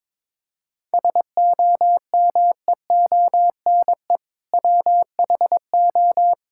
Morse code, heard as a keyed tone, about 22 wpm, 700 Hz. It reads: SOMEONE WHO
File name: Sentences with 7-Word Phrases - 22wpm